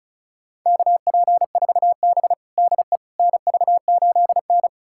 Morse code, heard as a keyed tone, about 36 wpm, 700 Hz. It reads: KP4B DE NV8N
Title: USA Callsign DE Callsign - 4 Characters - 35wpm